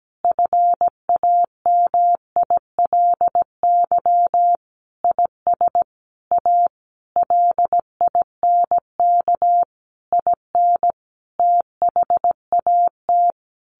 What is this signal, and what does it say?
Morse code, heard as a keyed tone, about 17 wpm, 700 Hz. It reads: FAMILY IS A LINK IN THAT
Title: Sentences from Top 1500 Words 17wpm